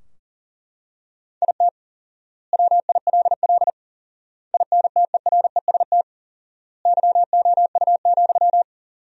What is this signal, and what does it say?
Morse code, heard as a keyed tone, about 40 wpm, 700 Hz. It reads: IT WILL INTEREST YOU,
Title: 4x Word-Spacing - Sentences from Top 400 Words - 40wpm